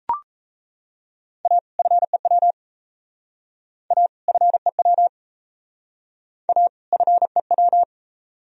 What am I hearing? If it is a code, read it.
Morse code, heard as a keyed tone, about 35 wpm, 700 Hz. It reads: A FEW A FEW A FEW